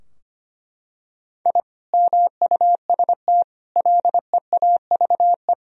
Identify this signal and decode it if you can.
Morse code, heard as a keyed tone, about 25 wpm, 700 Hz. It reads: I MUST LEAVE
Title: Sentences from Top 1000 Words - Review Part 2 - 25wpm